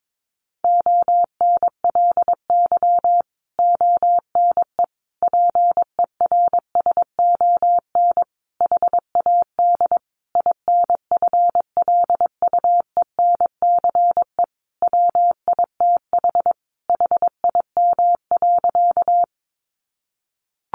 Morse code, heard as a keyed tone, about 22 wpm, 700 Hz. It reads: ONLY ONE PERSON HAD INFLUENCE WITH HIM.
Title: Sentences from Top 1100 Words 22wpm